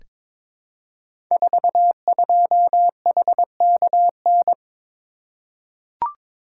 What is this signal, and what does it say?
Morse code, heard as a keyed tone, about 23 wpm, 700 Hz. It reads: 42HKN
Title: Code Groups - 5 Letter:Numbers - 22wpm